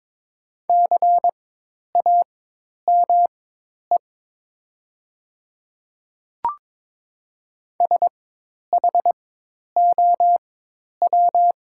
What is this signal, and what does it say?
Morse code, heard as a keyed tone, about 22 wpm, spaced as for 10 wpm, 700 Hz. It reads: CAME SHOW
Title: 4x Character Spacing - Top 200 Words - 22wpm